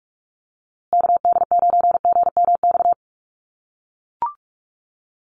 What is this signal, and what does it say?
Morse code, heard as a keyed tone, about 44 wpm, 700 Hz. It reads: KD9GMX
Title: USA Call Signs Repeated 3x 45wpm